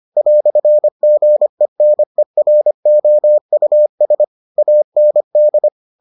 Morse code, heard as a keyed tone, about 25 wpm, 600 Hz. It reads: "GENEROUS AND